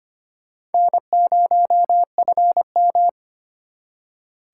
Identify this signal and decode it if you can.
Morse code, heard as a keyed tone, about 32 wpm, 700 Hz. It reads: N0FM